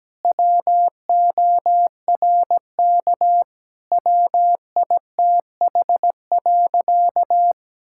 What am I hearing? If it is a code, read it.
Morse code, heard as a keyed tone, about 17 wpm, 700 Hz. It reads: WORK WITH.
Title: Sentences from Top 1000 Words - Review Part 6 - 17wpm